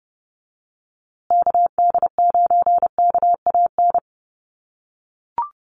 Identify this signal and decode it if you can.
Morse code, heard as a keyed tone, about 29 wpm, 700 Hz. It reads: KD9KAN
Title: USA Call Signs Repeated 3x 30wpm